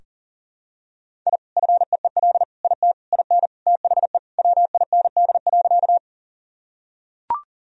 Morse code, heard as a keyed tone, about 37 wpm, 700 Hz. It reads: I FEEL IT IN THE WIND.
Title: Sentences from Top 1000 Words - Review Part 2 - 40wpm